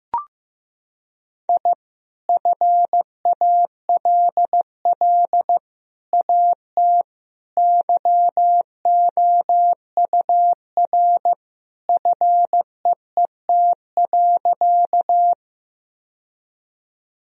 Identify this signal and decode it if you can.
Morse code, heard as a keyed tone, about 16 wpm, 700 Hz. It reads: I FALL AT YOUR FEET.
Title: Sentences from Top 1000 Words - Review Part 2 - 15wpm